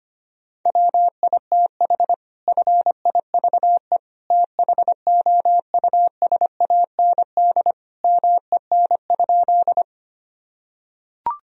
Morse code, heard as a keyed tone, about 25 wpm, 700 Hz. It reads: WITH FIVE THOUSAND MEN?
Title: Sentences from Top 1000 Words - Review Part 3 - 25wpm